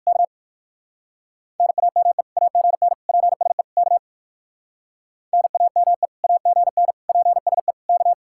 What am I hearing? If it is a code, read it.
Morse code, heard as a keyed tone, about 56 wpm, 700 Hz. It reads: K NAME AGN PSE K NAME AGN PSE K
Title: Speed-Racing Common QSO Phrases Encore - 45wpm